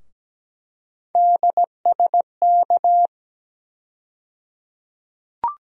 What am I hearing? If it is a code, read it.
Morse code, heard as a keyed tone, about 17 wpm, 700 Hz. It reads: DSK
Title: USA Call Sign Postfixes - 3 Letters 17wpm